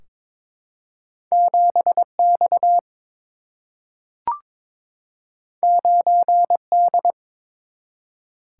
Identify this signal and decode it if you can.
Morse code, heard as a keyed tone, about 21 wpm, 700 Hz. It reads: 7 X 9 D
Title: Single-Letter Single-Number 22wpm